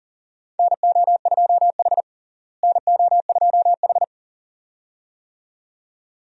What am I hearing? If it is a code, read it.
Morse code, heard as a keyed tone, about 40 wpm, 700 Hz. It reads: NO2H NO2H
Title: USA Call Signs - 4 Characters - Repeated 2x - 40wpm